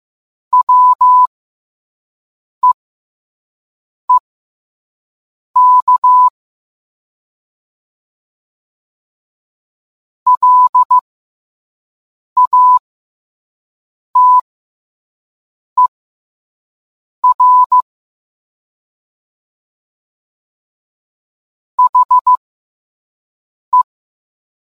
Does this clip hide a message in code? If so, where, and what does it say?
Yes: Morse code from 0.5 to 23.8 s, heard as a keyed tone, about 15 wpm, spaced as for 5 wpm, 1000 Hz. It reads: WEEK LATER HE